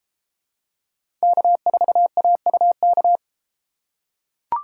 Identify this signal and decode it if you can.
Morse code, heard as a keyed tone, about 31 wpm, 700 Hz. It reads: K4AUK